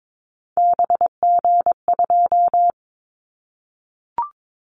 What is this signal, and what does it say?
Morse code, heard as a keyed tone, about 22 wpm, 700 Hz. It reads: BG2
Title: Callsign Prefixes for Top 100 DXCC - 22wpm